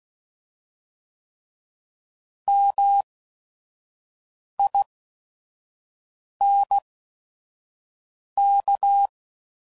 Morse code, heard as a keyed tone, about 16 wpm, spaced as for 5 wpm, 800 Hz. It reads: MINK